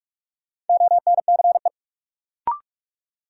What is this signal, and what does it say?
Morse code, heard as a keyed tone, about 46 wpm, 700 Hz. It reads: ONCE
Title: Sets of 4 Words - New Word Drill - 45wpm